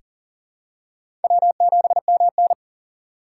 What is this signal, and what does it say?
Morse code, heard as a keyed tone, about 37 wpm, 700 Hz. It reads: W7MN
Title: USA Call Signs - 6x Character Spacing - 40wpm